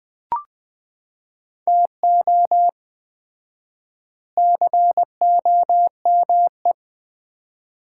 Morse code, heard as a keyed tone, about 19 wpm, 700 Hz. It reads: TO COME
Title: 4x Word-Spacing - Sets of 5 Words - 20wpm